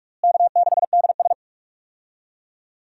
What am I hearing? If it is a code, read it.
Morse code, heard as a keyed tone, about 46 wpm, 700 Hz. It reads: K6DS
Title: USA Call Signs - 4 Characters - Repeated 2x - 45wpm